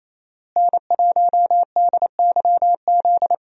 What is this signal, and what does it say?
Morse code, heard as a keyed tone, about 28 wpm, 700 Hz. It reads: N1DYZ
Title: USA Call Signs - 5 Characters - Repeated 3x - 28wpm